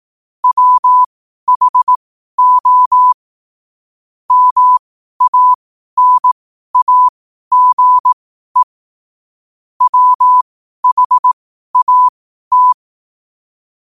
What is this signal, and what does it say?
Morse code, heard as a keyed tone, about 18 wpm, spaced as for 12 wpm, 1000 Hz. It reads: WHO MANAGE WHAT
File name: qotdfiles / 12wpm